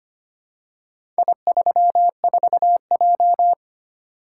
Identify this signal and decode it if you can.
Morse code, heard as a keyed tone, about 25 wpm, 700 Hz. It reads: I34J
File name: practice / 25wpm / Four Letter-Number Combinations - 25wpm